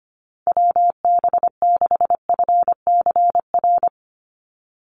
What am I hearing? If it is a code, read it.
Morse code, heard as a keyed tone, about 25 wpm, 700 Hz. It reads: WB6FCR
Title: USA Call Signs 25wpm